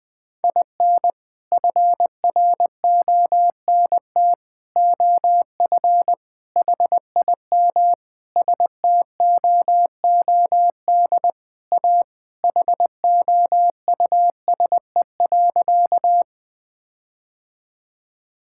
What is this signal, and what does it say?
Morse code, heard as a keyed tone, about 20 wpm, 700 Hz. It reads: IN FRONT OF HIM STOOD A HOUSE.
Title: Sentences with 4-Word Phrases - 20wpm